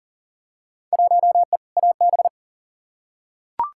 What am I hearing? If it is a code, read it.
Morse code, heard as a keyed tone, about 39 wpm, 700 Hz. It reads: 1E AB
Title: Field Day Exchanges - 40wpm